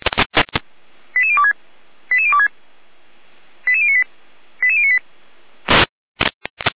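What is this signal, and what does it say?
Five-tone selective call (ZVEI), heard as quick runs of five tones, tones of about 70 ms. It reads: 88026 88026 88098 88098
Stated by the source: ZVEI five-tone call received by SDR.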